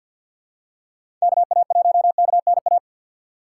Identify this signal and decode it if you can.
Morse code, heard as a keyed tone, about 47 wpm, 700 Hz. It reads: KA1KNA
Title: USA Call Signs - 6 Characters - 3x Character Spacing - 50wpm